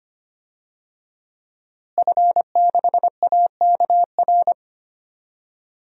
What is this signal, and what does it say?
Morse code, heard as a keyed tone, about 25 wpm, 700 Hz. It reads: F6AKR
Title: DXCC-100 Call Signs - 5 Characters - Repeated 2x - 25wpm